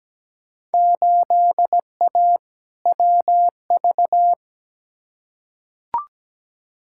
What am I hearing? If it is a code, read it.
Morse code, heard as a keyed tone, about 17 wpm, 700 Hz. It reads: 8A WV
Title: Field Day Exchanges Encore - 17wpm